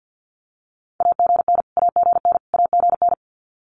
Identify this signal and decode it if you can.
Morse code, heard as a keyed tone, about 50 wpm, 700 Hz. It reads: AGN AGN AGN
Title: Abbreviations Repeated 3x - 50wpm